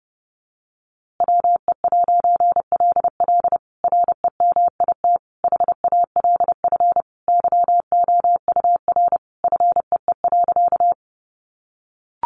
Morse code, heard as a keyed tone, about 30 wpm, 700 Hz. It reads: WE'LL REMIT HALF YOUR FEE.
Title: A Sound Of Thunder 30wpm